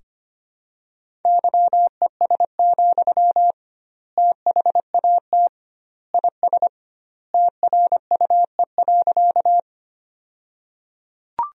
Morse code, heard as a keyed tone, about 25 wpm, 700 Hz. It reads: YES, THAT IS TRUE.
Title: 2x Word-Spacing - Sentences from Top 500 Words - 25wpm